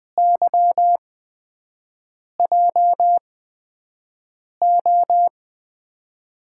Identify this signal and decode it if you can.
Morse code, heard as a keyed tone, about 20 wpm, spaced as for 5 wpm, 700 Hz. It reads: YJO
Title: USA Call Signs - 5 Characters - 8x Character Spacing - 20wpm